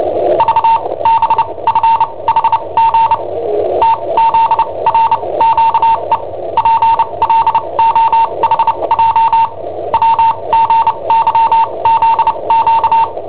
Morse code, heard as a keyed tone, about 29 wpm, spaced as for 23 wpm, 910 Hz. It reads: V BFHG TZRQE PLOHJ WGYZQ
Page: Cassetta esercizi preparata da I7OHP per IW5ELP